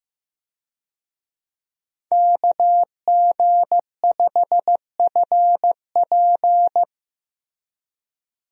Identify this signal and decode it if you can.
Morse code, heard as a keyed tone, about 15 wpm, 700 Hz. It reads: KG5FP
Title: USA Call Signs - 5 Characters - Repeated 3x - 15wpm